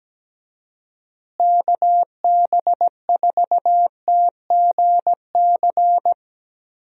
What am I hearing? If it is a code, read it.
Morse code, heard as a keyed tone, about 17 wpm, 700 Hz. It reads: KB4TGC
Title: USA Call Signs - 6 Characters - Repeated 2x - 17wpm